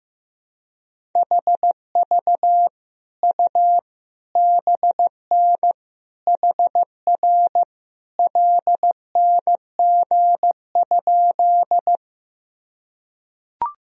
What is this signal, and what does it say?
Morse code, heard as a keyed tone, about 15 wpm, 700 Hz. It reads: HV U BN HR LNG?
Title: Sentences from Top 200 Words and Abbreviations 15wpm